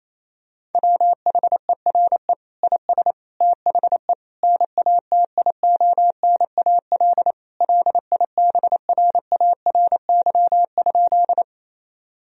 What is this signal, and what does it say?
Morse code, heard as a keyed tone, about 28 wpm, 700 Hz. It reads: WHERE IS THE NATIONAL LIBRARY?
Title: Sentences from Top 1000 Words - Review Part 4 - 28wpm